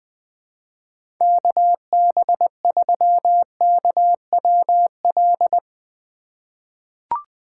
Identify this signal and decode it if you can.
Morse code, heard as a keyed tone, about 20 wpm, 700 Hz. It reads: KB3KWL
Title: USA Call Signs - 4x Character Spacing - 20wpm